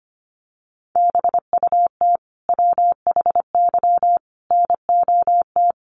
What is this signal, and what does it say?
Morse code, heard as a keyed tone, about 25 wpm, 700 Hz. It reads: BUT WHY NOT
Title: Sentences from Top 1400 Words 25wpm